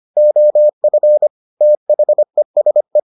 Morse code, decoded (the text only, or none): OF THESE